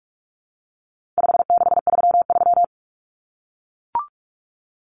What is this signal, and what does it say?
Morse code, heard as a keyed tone, about 44 wpm, 700 Hz. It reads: 5633
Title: Numbers - 4-Digit 45wpm